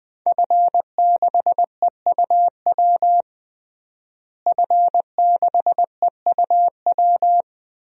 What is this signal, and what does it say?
Morse code, heard as a keyed tone, about 21 wpm, 700 Hz. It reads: F6EUW F6EUW
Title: DXCC-100 Call Signs - 5 Characters - Repeated 2x - 20wpm